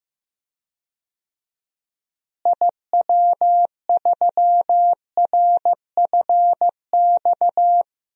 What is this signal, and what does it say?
Morse code, heard as a keyed tone, about 15 wpm, 700 Hz. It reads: IW3RFX
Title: DXCC-100 Call Signs - 6 Characters - Repeated 2x - 15wpm